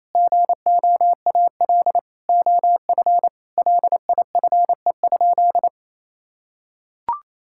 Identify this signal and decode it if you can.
Morse code, heard as a keyed tone, about 28 wpm, 700 Hz. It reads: GOAL OF LIFE?
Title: Sentences from Top 1000 Words - Review Part 5 - 28wpm